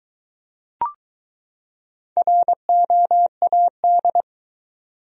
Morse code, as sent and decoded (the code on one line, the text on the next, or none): .-. --- .- -..
ROAD